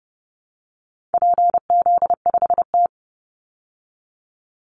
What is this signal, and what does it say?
Morse code, heard as a keyed tone, about 30 wpm, 700 Hz. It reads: PZ5T
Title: Call Signs Repeated 2x - 30wpm